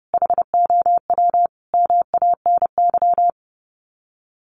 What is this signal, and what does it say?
Morse code, heard as a keyed tone, about 30 wpm, 700 Hz. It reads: HOW MANY
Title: Sets of 5 Words Encore 30wpm